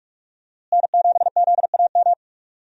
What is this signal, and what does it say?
Morse code, heard as a keyed tone, about 46 wpm, 700 Hz. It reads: N7ZAM
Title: USA Call Signs - 5 Characters - Repeated 3x - 45wpm